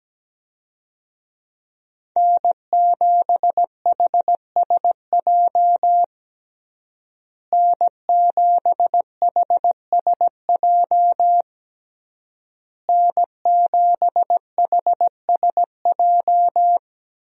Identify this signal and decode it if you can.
Morse code, heard as a keyed tone, about 17 wpm, 700 Hz. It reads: N7HSJ N7HSJ N7HSJ
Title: USA Call Signs - 5 Characters - Repeated 3x - 17wpm